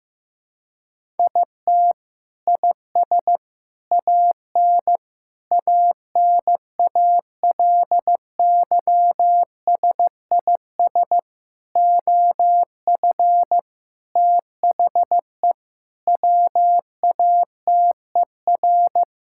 Morse code, heard as a keyed tone, about 15 wpm, 700 Hz. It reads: IT IS AN ANALYSIS OF THE WATER
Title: Sentences from Top 1000 Words - Review Part 5 - 15wpm